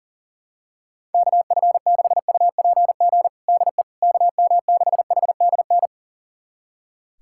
Morse code, heard as a keyed tone, about 40 wpm, 700 Hz. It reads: KF6UPG DE KM6HDN
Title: USA Callsign DE Callsign - 6 Characters - 40wpm